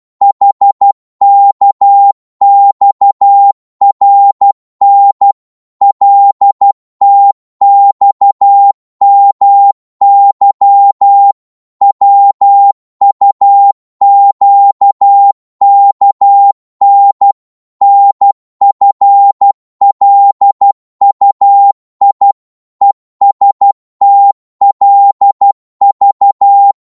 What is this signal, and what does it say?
Morse code, heard as a keyed tone, about 12 wpm, 800 Hz. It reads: HKXRN LTXMY WUQKN NFLUI ESTLV
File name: Morse / ltr-12wpm